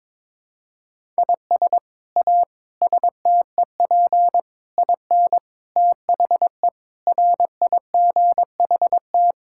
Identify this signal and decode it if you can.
Morse code, heard as a keyed tone, about 22 wpm, 700 Hz. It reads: IS A STEP IN THE RIGHT
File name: practice / 22wpm / Sets of 8 Words - 22wpm